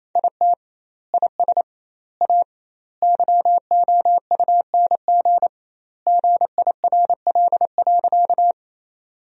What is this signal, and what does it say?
Morse code, heard as a keyed tone, about 28 wpm, 700 Hz. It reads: IT IS A YOUNG GIRL.
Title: 2x Word-Spacing - Sentences from Top 400 Words - 28wpm